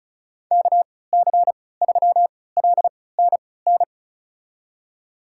Morse code, heard as a keyed tone, about 36 wpm, spaced as for 20 wpm, 700 Hz. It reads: KC3LNN
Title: USA Call Signs - 6 Characters - 3x Character Spacing - 35wpm